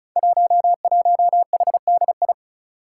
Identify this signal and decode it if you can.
Morse code, heard as a keyed tone, about 36 wpm, 700 Hz. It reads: 11HDI
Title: Five Challenging Letter-Number Combinations 35wpm